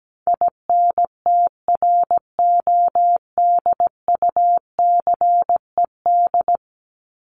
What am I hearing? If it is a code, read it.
Morse code, heard as a keyed tone, about 17 wpm, 700 Hz. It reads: INTRODUCED
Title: ED Suffix - Top 100 Words with Base Words 17wpm